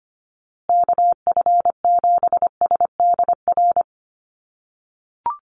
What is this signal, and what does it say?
Morse code, heard as a keyed tone, about 25 wpm, 700 Hz. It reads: KF7SDR
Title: USA Call Signs Repeated 2x 25wpm